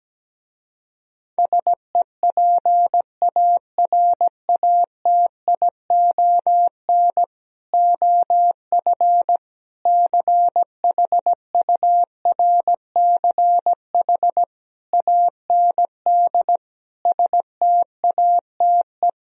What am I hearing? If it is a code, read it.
Morse code, heard as a keyed tone, about 17 wpm, 700 Hz. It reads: SEPARATION OF CHURCH AND STATE
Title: Speed-Racing - Sets of 5 Words 17wpm